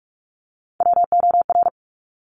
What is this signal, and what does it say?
Morse code, heard as a keyed tone, about 46 wpm, 700 Hz. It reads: WOR
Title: Top Three Letter Combinations in Words 45wpm